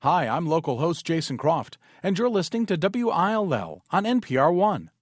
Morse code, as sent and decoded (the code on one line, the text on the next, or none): none